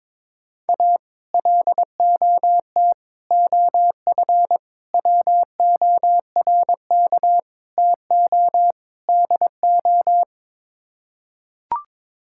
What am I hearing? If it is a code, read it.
Morse code, heard as a keyed tone, about 22 wpm, 700 Hz. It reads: A LOT OF WORK TO DO
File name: practice / 22wpm / Sets of 6 Words - 22wpm